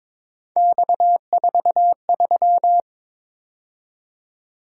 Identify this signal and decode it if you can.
Morse code, heard as a keyed tone, about 26 wpm, 700 Hz. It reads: X43
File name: Speed-Racing - Three Letter-Number Combinations - 22wpm